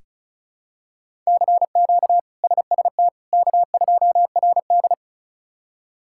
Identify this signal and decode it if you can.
Morse code, heard as a keyed tone, about 36 wpm, 700 Hz. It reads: CQ SST K2RD
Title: K1USN SST QSO Practice - 35wpm